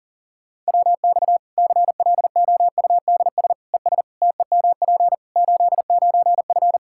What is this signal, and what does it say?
Morse code, heard as a keyed tone, about 40 wpm, 700 Hz. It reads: WX CLOUDS ES TEMP 89F
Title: Practice QSO - Exchange 2 - Info - 40wpm